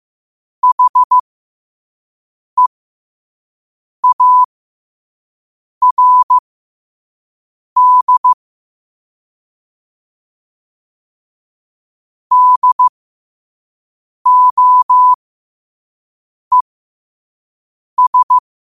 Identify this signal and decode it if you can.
Morse code, heard as a keyed tone, about 15 wpm, spaced as for 5 wpm, 1000 Hz. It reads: HEARD DOES
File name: qotdfiles / 5wpm